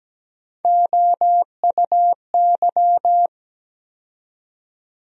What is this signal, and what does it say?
Morse code, heard as a keyed tone, about 17 wpm, 700 Hz. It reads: OUY
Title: USA Call Sign Postfixes - 3 Letters 17wpm